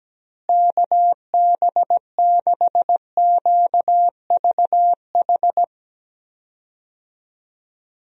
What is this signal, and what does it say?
Morse code, heard as a keyed tone, about 17 wpm, 700 Hz. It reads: KB6QVH
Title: USA Call Signs - 6 Characters - Repeated 3x - 17wpm